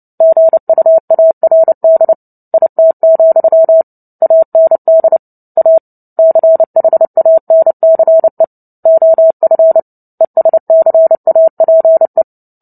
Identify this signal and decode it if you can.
Morse code, heard as a keyed tone, about 30 wpm, 650 Hz. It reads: GUARD IT, AND A CHANCE OF ESCAPE
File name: Book Two / 03-Chapter 3 - 29 WPM